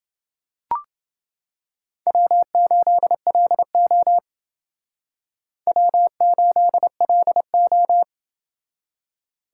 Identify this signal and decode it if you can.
Morse code, heard as a keyed tone, about 29 wpm, 700 Hz. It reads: W8LO W8LO